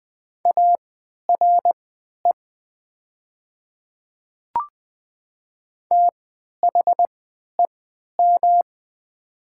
Morse code, heard as a keyed tone, about 20 wpm, spaced as for 11 wpm, 700 Hz. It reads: ARE THEM
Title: 3x Character Spacing - Top 100 Words - 20wpm